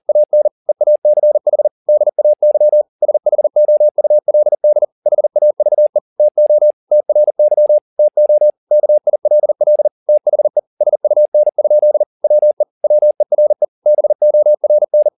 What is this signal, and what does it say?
Morse code, heard as a keyed tone, about 40 wpm, 600 Hz. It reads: AN EACH DAY SHOULD HAVE TO TRY TO KILL THE SUN? WE WERE BORN
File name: the old man and the sea 2 40wpm